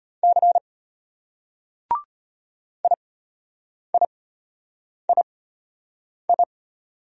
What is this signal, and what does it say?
Morse code, heard as a keyed tone, about 34 wpm, spaced as for 8 wpm, 700 Hz. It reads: C IIII